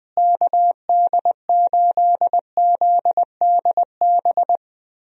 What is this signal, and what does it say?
Morse code, heard as a keyed tone, about 20 wpm, 700 Hz. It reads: KD8ZDB
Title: USA Call Signs - 6 Characters - Repeated 3x - 20wpm